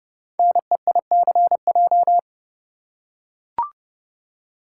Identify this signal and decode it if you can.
Morse code, heard as a keyed tone, about 30 wpm, 700 Hz. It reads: NEICJ
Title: Code Groups - 5 Letters - 30wpm